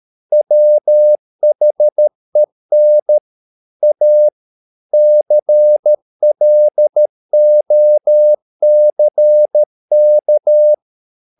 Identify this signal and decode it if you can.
Morse code, heard as a keyed tone, about 13 wpm, 600 Hz. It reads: WHEN A CLOCK
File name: mp3 files / Lexiphiles - 13wpm